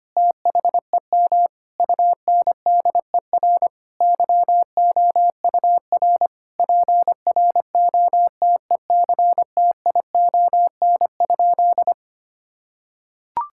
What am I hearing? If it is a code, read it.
Morse code, heard as a keyed tone, about 25 wpm, 700 Hz. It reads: THEM UNDER YOUR PROTECTION?
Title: Sentences from Top 1600 Words 25wpm